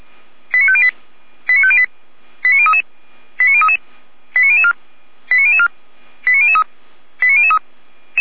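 Five-tone selective call (ZVEI), heard as quick runs of five tones, tones of about 70 ms. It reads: 89508 89508 89033 89033 89004 89004 89003 89003
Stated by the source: ZVEI five-tone selective call 89010, received by SDR, on the Feuerwehr Orlamünde audio server.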